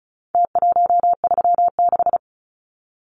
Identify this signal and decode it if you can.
Morse code, heard as a keyed tone, about 34 wpm, 700 Hz. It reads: T136
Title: Numbers - Contest Style - 4-Digit 35wpm